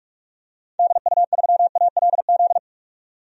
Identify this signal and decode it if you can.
Morse code, heard as a keyed tone, about 44 wpm, 700 Hz. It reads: DU3ALZ
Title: DXCC-100 Call Signs - 6 Characters - 45wpm